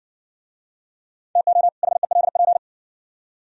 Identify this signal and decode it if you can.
Morse code, heard as a keyed tone, about 59 wpm, 690 Hz. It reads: TO HELP